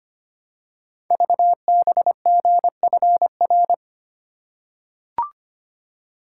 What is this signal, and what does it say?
Morse code, heard as a keyed tone, about 25 wpm, 700 Hz. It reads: VBGFR
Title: Code Groups - 5 Letters - 25wpm